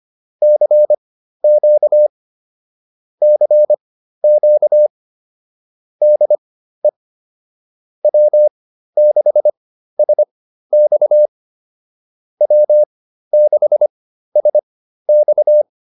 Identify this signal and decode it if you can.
Morse code, heard as a keyed tone, about 25 wpm, spaced as for 13 wpm, 600 Hz. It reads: CQ CQ DE W6SX W6SX